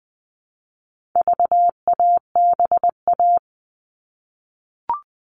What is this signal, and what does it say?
Morse code, heard as a keyed tone, about 20 wpm, 700 Hz. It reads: VABA
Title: All Cut Numbers - 4-Digit 20wpm